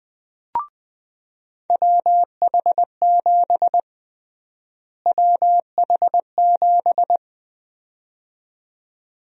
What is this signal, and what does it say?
Morse code, heard as a keyed tone, about 20 wpm, 700 Hz. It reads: WH7 WH7
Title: USA Call Sign Prefixes - Repeated 2x - 20wpm